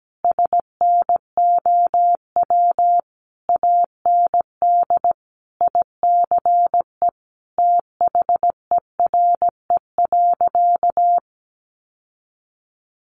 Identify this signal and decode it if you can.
Morse code, heard as a keyed tone, about 17 wpm, 700 Hz. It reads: SNOW AND ICE THERE.
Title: Sentences from Top 1000 Words 17wpm